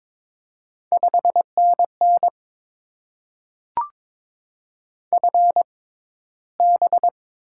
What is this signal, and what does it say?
Morse code, heard as a keyed tone, about 22 wpm, spaced as for 14 wpm, 700 Hz. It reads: 5NN FB
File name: Top 20 QSO Elements - 6x Character Spacing - 22wpm